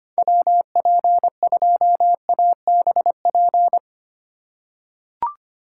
Morse code, heard as a keyed tone, about 25 wpm, 700 Hz. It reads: WP2ABP
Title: USA Call Signs - 6 Characters - 2x Character Spacing - 25wpm